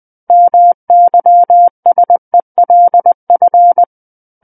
Morse code, heard as a keyed tone, about 20 wpm, 700 Hz. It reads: MYSELF
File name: 20 WPM